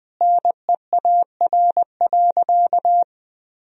Morse code, heard as a keyed tone, about 20 wpm, 700 Hz. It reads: NEAR.
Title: Sentences from Top 300 Words and Abbreviations 20wpm